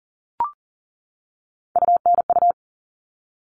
Morse code, heard as a keyed tone, about 40 wpm, 700 Hz. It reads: UNU
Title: All Cut Numbers - 3-Digit 40wpm